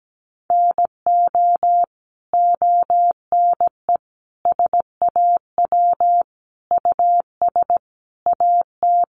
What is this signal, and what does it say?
Morse code, heard as a keyed tone, about 17 wpm, 700 Hz. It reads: NO ONE SAW US AT
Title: Sentences from Top 1200 Words 17wpm